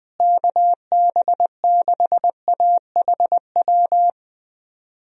Morse code, heard as a keyed tone, about 20 wpm, 700 Hz. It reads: KB6AHW
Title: USA Call Signs - 6 Characters - Repeated 3x - 20wpm